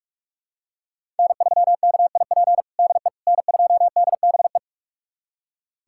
Morse code, heard as a keyed tone, about 46 wpm, 700 Hz. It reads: N3KIP DE N2DBE
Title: USA Callsign DE Callsign - 5 Characters - 45wpm